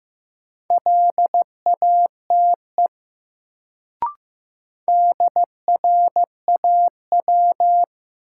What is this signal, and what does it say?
Morse code, heard as a keyed tone, about 15 wpm, 700 Hz. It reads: LATE DRAW
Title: Sentences from Top 300 Words - New Word Drill - 15wpm